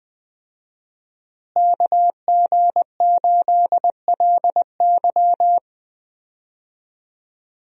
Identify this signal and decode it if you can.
Morse code, heard as a keyed tone, about 21 wpm, 700 Hz. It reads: KG8LY
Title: USA Call Signs - 5 Characters - Repeated 3x - 20wpm